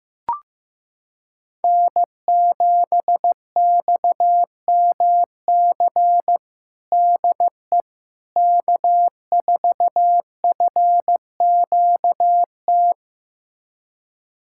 Morse code, heard as a keyed tone, about 15 wpm, 700 Hz. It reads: N7XMC DE K4FQT
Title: USA Callsign DE Callsign 15wpm